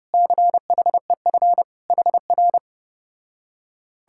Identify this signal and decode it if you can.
Morse code, heard as a keyed tone, about 33 wpm, 700 Hz. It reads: CHEF HR